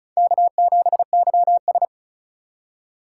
Morse code, heard as a keyed tone, about 36 wpm, 700 Hz. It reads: K7YS
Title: USA Call Signs - 4 Characters - 4x Character Spacing - 35wpm